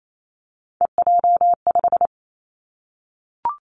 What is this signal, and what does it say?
Morse code, heard as a keyed tone, about 28 wpm, 700 Hz. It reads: EJ5
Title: Three Challenging Letter-Number Combinations 28wpm